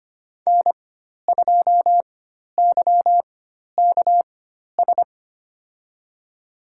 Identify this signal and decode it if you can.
Morse code, heard as a keyed tone, about 25 wpm, spaced as for 12 wpm, 700 Hz. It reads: N2YKS
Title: USA Call Signs - 5 Characters - 4x Character Spacing - 25wpm